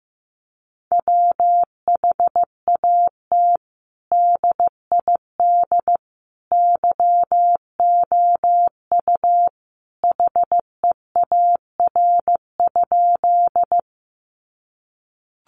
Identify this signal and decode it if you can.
Morse code, heard as a keyed tone, about 15 wpm, 700 Hz. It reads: WHAT DID YOU HEAR?
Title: Sentences from Top 300 Words 15wpm